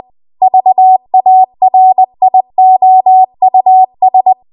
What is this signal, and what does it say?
Morse code, heard as a keyed tone, about 20 wpm, 750 Hz. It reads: VARIOUS
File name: Archive / 20 WPM / 20WPM